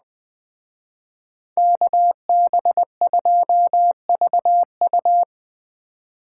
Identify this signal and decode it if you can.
Morse code, heard as a keyed tone, about 20 wpm, 700 Hz. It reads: KB2VU